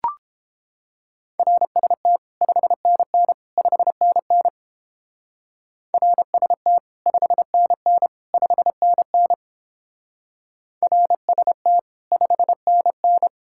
Morse code, heard as a keyed tone, about 29 wpm, 700 Hz. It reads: RST 5NN 5NN RST 5NN 5NN RST 5NN